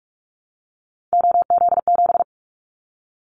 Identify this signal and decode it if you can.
Morse code, heard as a keyed tone, about 46 wpm, 700 Hz. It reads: OZ7
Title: Callsign Prefixes for Top 100 DXCC - 45wpm